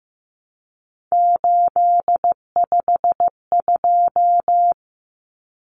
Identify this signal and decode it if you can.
Morse code, heard as a keyed tone, about 15 wpm, 700 Hz. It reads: 852
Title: Common Cut Numbers - 3-Digit 15wpm